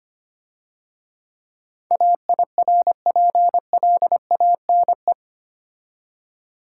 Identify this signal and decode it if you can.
Morse code, heard as a keyed tone, about 29 wpm, 700 Hz. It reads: AIRPLANE
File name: Easy Words - Speed-Racing with Hints - 25wpm